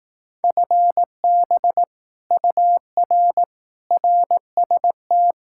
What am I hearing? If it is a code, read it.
Morse code, heard as a keyed tone, about 18 wpm, 700 Hz. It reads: FB UR RST
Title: Speed-Racing Common QSO Phrases Encore - 15wpm